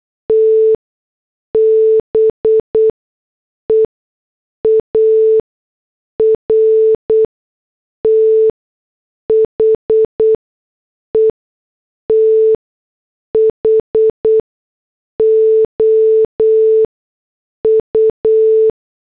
Morse code, heard as a keyed tone, about 8 wpm, spaced as for 6 wpm, 440 Hz. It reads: T BEAR THE THOU